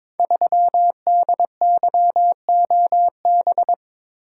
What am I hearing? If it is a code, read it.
Morse code, heard as a keyed tone, about 22 wpm, 700 Hz. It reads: 3DYOB
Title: Code Groups - 5 Letter:Numbers - 22wpm